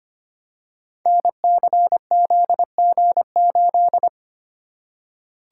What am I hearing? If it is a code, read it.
Morse code, heard as a keyed tone, about 25 wpm, 700 Hz. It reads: NCZG8
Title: Code Groups - 5 Letter:Numbers - 25wpm